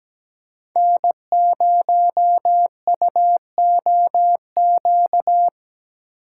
Morse code, heard as a keyed tone, about 17 wpm, 700 Hz. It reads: N0UOQ
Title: USA Call Signs - 5 Characters - 8x Character Spacing - 17wpm